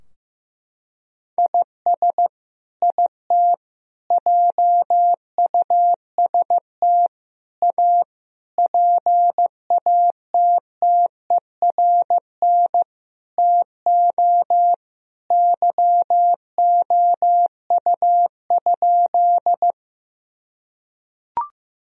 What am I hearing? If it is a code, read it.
Morse code, heard as a keyed tone, about 15 wpm, 700 Hz. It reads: IS IT JUST A PATTERN TO YOU?
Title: Sentences from Top 1000 Words - Review Part 2 - 15wpm